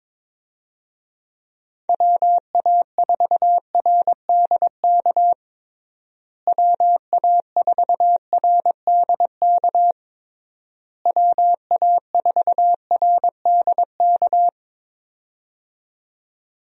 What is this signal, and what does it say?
Morse code, heard as a keyed tone, about 22 wpm, 700 Hz. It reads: WA4RDK WA4RDK WA4RDK
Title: USA Call Signs - 6 Characters - Repeated 3x - 22wpm